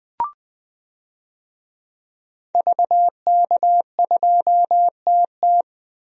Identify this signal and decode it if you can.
Morse code, heard as a keyed tone, about 21 wpm, 700 Hz. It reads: VK2TT
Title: DXCC-100 Call Signs - 5 Characters - Repeated 3x - 20wpm